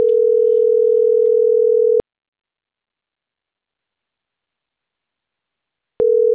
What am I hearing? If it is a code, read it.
Ringback tone (North America), 440 + 480 Hz.